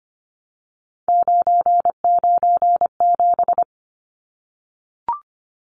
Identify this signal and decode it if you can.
Morse code, heard as a keyed tone, about 25 wpm, 700 Hz. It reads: 997
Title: Numbers - 3-Digit 25wpm